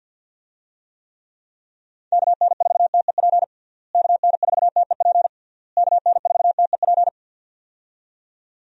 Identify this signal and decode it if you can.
Morse code, heard as a keyed tone, about 50 wpm, 700 Hz. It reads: KN4TEP KN4TEP KN4TEP
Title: USA Call Signs - 6 Characters - Repeated 3x - 50wpm